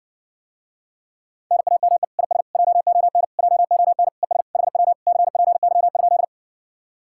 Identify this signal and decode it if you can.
Morse code, heard as a keyed tone, about 60 wpm, 700 Hz. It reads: NAME IS JON JON ES HW CPY?